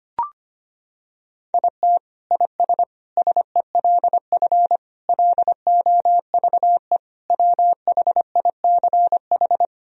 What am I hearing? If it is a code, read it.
Morse code, heard as a keyed tone, about 25 wpm, 700 Hz. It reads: IT IS SELF LOVE WHICH
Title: Sentences from Top 1000 Words - Review Part 1 - 25wpm